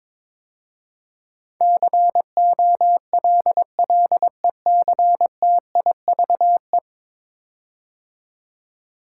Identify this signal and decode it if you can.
Morse code, heard as a keyed tone, about 22 wpm, 700 Hz. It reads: COLLECTIVE
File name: practice / 22wpm / 10 Letter Words - First 3 Letters Given - 22wpm